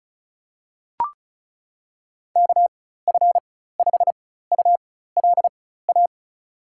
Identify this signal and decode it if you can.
Morse code, heard as a keyed tone, about 36 wpm, spaced as for 16 wpm, 700 Hz. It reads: KF5ULA
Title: USA Call Signs - 4x Character Spacing - 35wpm